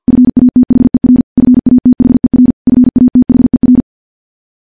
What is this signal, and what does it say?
Morse code, heard as a keyed tone, about 50 wpm, 250 Hz. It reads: FATHER FATHER FATHER
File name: morse / morse-training-50WPM